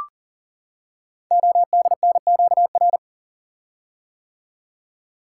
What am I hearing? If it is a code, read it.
Morse code, heard as a keyed tone, about 38 wpm, 690 Hz. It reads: ODNQR
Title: Code Groups - 5 Letters - 40wpm